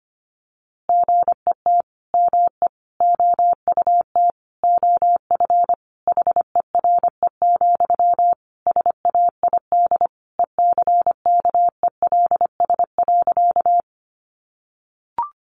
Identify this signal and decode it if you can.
Morse code, heard as a keyed tone, about 25 wpm, 700 Hz. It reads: GET ME OUT OF HERE, SAID ECKELS.
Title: A Sound Of Thunder 25wpm